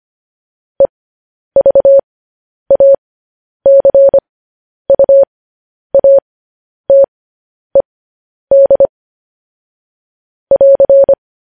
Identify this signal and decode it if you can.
Morse code, heard as a keyed tone, about 25 wpm, spaced as for 10 wpm, 550 Hz. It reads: EVACUATED <AR>
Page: news headlines in morse code at 10 wpm